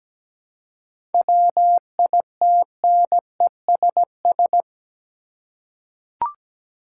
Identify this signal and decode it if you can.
Morse code, heard as a keyed tone, about 17 wpm, 700 Hz. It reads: WITNESS
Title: Seven Letter Words - 17wpm